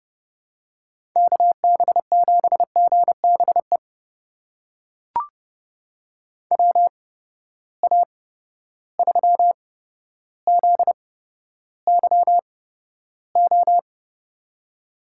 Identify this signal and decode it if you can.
Morse code, heard as a keyed tone, about 30 wpm, spaced as for 13 wpm, 700 Hz. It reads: KB7GBE WA3ZYO
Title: USA Call Signs - 6 Characters - 8x Character Spacing - 30wpm